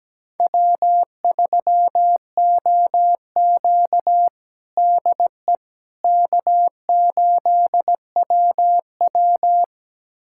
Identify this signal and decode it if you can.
Morse code, heard as a keyed tone, about 17 wpm, 700 Hz. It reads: W3OQ DE K8WW
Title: USA Callsign DE Callsign - 4 Characters - 17wpm